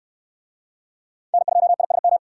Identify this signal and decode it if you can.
Morse code, heard as a keyed tone, about 69 wpm, 700 Hz. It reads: N2ISM